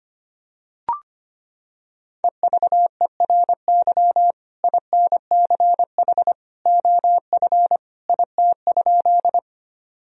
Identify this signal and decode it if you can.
Morse code, heard as a keyed tone, about 25 wpm, 700 Hz. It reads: EVERY INCH OF IT?
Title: Sentences from Top 1000 Words - Review Part 3 - 25wpm